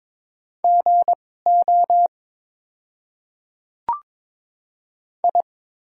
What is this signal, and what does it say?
Morse code, heard as a keyed tone, about 22 wpm, spaced as for 16 wpm, 700 Hz. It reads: GO I
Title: 2x Character Spacing - Top 100 Words - 22wpm